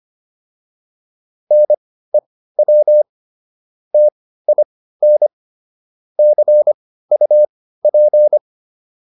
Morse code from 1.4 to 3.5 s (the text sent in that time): NEW